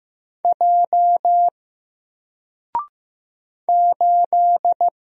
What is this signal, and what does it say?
Morse code, heard as a keyed tone, about 15 wpm, 700 Hz. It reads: J 8
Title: Single Letter-Number 15wpm